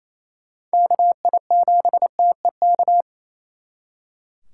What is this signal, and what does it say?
Morse code, heard as a keyed tone, about 36 wpm, 700 Hz. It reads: KI7TEK